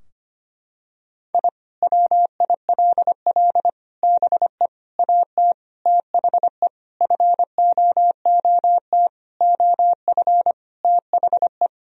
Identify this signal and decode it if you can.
Morse code, heard as a keyed tone, about 25 wpm, 700 Hz. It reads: I WILL BE AT THE FOOT OF THE
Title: Sentences from Top 1000 Words - Review Part 3 - 25wpm